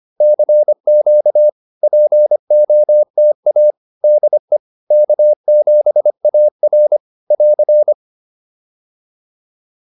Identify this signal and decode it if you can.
Morse code, heard as a keyed tone, about 25 wpm, 600 Hz. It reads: CQ POTA DE K7AR <AR>
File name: POTA309 25